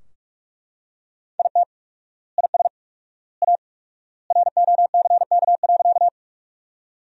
Morse code, heard as a keyed tone, about 41 wpm, 700 Hz. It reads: IT IS A ROCK.
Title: 4x Word-Spacing - Sentences from Top 400 Words - 45wpm